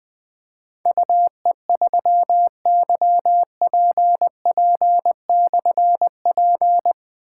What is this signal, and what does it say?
Morse code, heard as a keyed tone, about 20 wpm, 700 Hz. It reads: UE3YPP/P
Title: Difficult Call Signs 20wpm